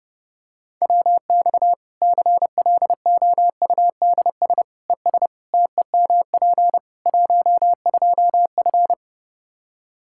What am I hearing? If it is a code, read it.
Morse code, heard as a keyed tone, about 32 wpm, 700 Hz. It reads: WX CLOUDS ES TEMP 12F
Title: Practice QSO - Exchange 2 - Info - 30wpm